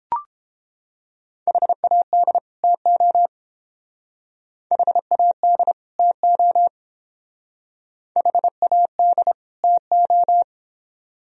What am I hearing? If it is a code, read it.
Morse code, heard as a keyed tone, about 28 wpm, 700 Hz. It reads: HAD TO HAD TO HAD TO